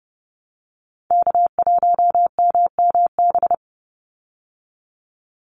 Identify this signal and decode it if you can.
Morse code, heard as a keyed tone, about 30 wpm, 700 Hz. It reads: K1MMB
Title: Call Signs Repeated 2x - 30wpm